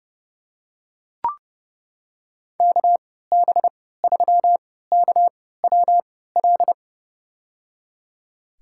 Morse code, heard as a keyed tone, about 30 wpm, spaced as for 17 wpm, 700 Hz. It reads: KB3KWL
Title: USA Call Signs - 6 Characters - 3x Character Spacing - 30wpm